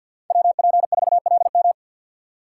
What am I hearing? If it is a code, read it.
Morse code, heard as a keyed tone, about 53 wpm, 700 Hz. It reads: WP4LM
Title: USA Call Signs - 3x Character Spacing - 50wpm